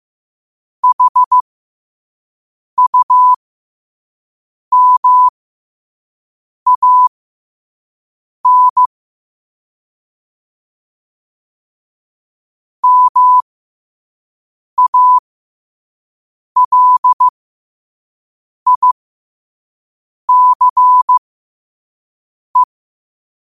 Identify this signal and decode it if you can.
Morse code, heard as a keyed tone, about 15 wpm, spaced as for 5 wpm, 1000 Hz. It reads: HUMAN MALICE
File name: qotdfiles / 5wpm